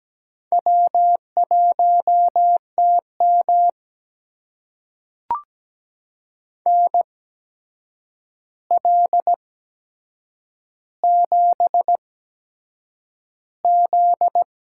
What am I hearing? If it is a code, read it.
Morse code, heard as a keyed tone, about 17 wpm, spaced as for 7 wpm, 700 Hz. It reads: W1TM NL7Z
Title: USA Call Signs - 4 Characters - 8x Character Spacing - 17wpm